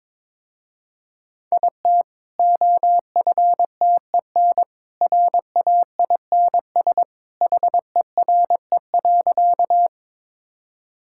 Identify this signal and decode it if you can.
Morse code, heard as a keyed tone, about 22 wpm, 700 Hz. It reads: IT OFTEN RAINS HERE.
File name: practice / 22wpm / Easy Sentences with Hints - 22wpm